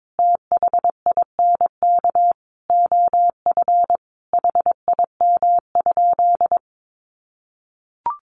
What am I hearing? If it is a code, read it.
Morse code, heard as a keyed tone, about 22 wpm, 700 Hz. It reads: THINK OF HIM?
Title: Sentences from Top 200 Words 22wpm